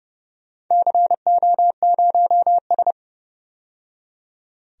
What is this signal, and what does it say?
Morse code, heard as a keyed tone, about 30 wpm, 700 Hz. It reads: CO0S
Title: DXCC-100 Call Signs - 4 Characters - 30wpm